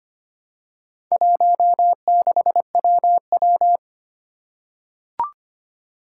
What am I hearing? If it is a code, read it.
Morse code, heard as a keyed tone, about 25 wpm, 700 Hz. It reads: 16WW
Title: Four Challenging Letter-Number Combinations 25wpm